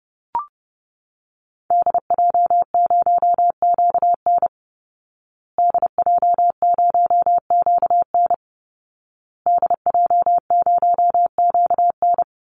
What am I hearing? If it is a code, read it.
Morse code, heard as a keyed tone, about 30 wpm, 700 Hz. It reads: DJ0QN DJ0QN DJ0QN
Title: Call Signs Repeated 2x - 30wpm